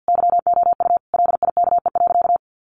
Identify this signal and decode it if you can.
Morse code, heard as a keyed tone, about 50 wpm, 700 Hz. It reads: YOU LIKE.
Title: Sentences from Top 500 Words 50wpm